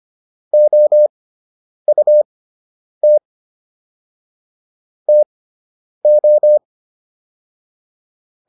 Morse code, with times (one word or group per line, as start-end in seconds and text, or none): 0.5-3.2 OUT
5.1-6.6 TO